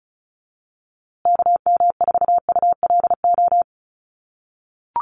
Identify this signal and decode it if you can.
Morse code, heard as a keyed tone, about 34 wpm, 700 Hz. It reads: KM4ULO
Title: USA Call Signs Repeated 3x 35wpm